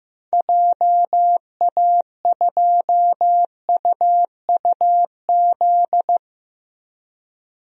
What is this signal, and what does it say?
Morse code, heard as a keyed tone, about 15 wpm, 700 Hz. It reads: JA2UUZ
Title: DXCC-100 Call Signs - 6 Characters - Repeated 3x - 15wpm